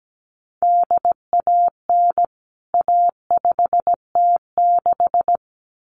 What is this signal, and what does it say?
Morse code, heard as a keyed tone, about 17 wpm, 700 Hz. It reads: DAN A5T6
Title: CWT Exchange 17wpm